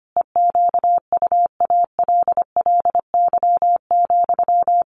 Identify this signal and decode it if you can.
Morse code, heard as a keyed tone, about 25 wpm, 700 Hz. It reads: EQUALLY,
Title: Sentences from Top 1400 Words 25wpm